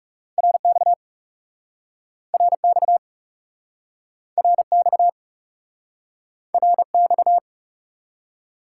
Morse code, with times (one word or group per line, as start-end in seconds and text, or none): 0.4-0.9 RX
2.3-3.0 RX
4.4-5.1 RX
6.5-7.4 RX